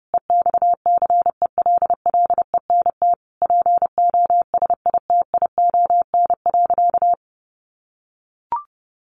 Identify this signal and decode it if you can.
Morse code, heard as a keyed tone, about 30 wpm, 700 Hz. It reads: EXCELLENT POSITION.
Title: Sentences from Top 1000 Words 30wpm